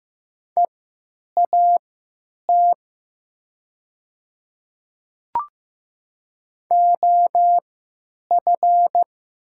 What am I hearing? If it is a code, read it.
Morse code, heard as a keyed tone, about 15 wpm, spaced as for 9 wpm, 700 Hz. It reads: EAT OF
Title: 3x Character Spacing - Top 500 Words - 15wpm